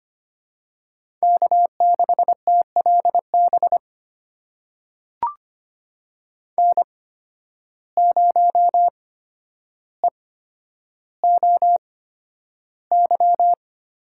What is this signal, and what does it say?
Morse code, heard as a keyed tone, about 25 wpm, spaced as for 11 wpm, 700 Hz. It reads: K6TLB N0EOY
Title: USA Call Signs - 5 Characters - 8x Character Spacing - 25wpm